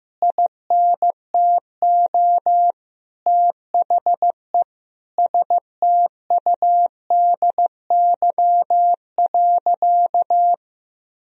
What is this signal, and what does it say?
Morse code, heard as a keyed tone, about 15 wpm, 700 Hz. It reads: INTO THE STUDY.
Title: Sentences from Top 1000 Words - Review Part 1 - 15wpm